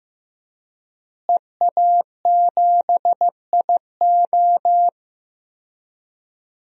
Morse code, heard as a keyed tone, about 15 wpm, 700 Hz. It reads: EA7IO
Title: DXCC-100 Call Signs - 5 Characters - Repeated 3x - 15wpm